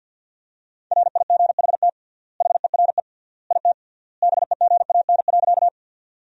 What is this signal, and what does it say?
Morse code, heard as a keyed tone, about 52 wpm, 700 Hz. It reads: RIGHT HERE IT BEGAN.
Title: 3x Word-Spacing - Sentences from Top 300 Words - 50wpm